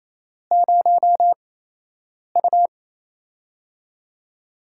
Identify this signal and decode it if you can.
Morse code, heard as a keyed tone, about 28 wpm, spaced as for 8 wpm, 700 Hz. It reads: 0U
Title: USA Call Signs - 4 Characters - 8x Character Spacing - 28wpm